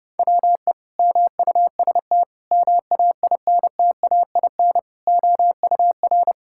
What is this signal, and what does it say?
Morse code, heard as a keyed tone, about 30 wpm, 700 Hz. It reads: WE MUST MAINTAIN OUR
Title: Sentences from Top 1000 Words - Review Part 5 - 30wpm